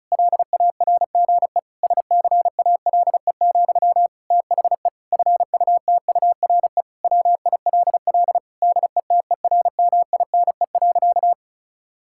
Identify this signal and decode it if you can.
Morse code, heard as a keyed tone, about 36 wpm, 700 Hz. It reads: LARGE SCALE, THE FUTURE WILL DETERMINE.
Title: Sentences from Top 1000 Words - Review Part 5 - 35wpm